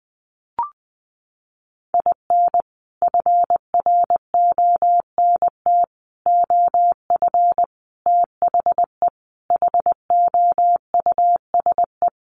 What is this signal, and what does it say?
Morse code, heard as a keyed tone, about 20 wpm, 700 Hz. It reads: IN FRONT OF THE HOUSE
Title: Sets of 5 Words Encore2 20wpm